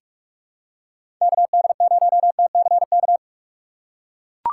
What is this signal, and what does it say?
Morse code, heard as a keyed tone, about 43 wpm, 700 Hz. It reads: KD0TCK
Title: USA Call Signs - 4x Character Spacing - 45wpm